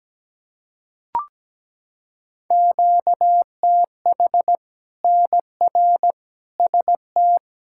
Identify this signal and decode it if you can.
Morse code, heard as a keyed tone, about 17 wpm, 700 Hz. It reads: QTH NR ST
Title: Practice QSO - Exchange 1 - Beginning - 17wpm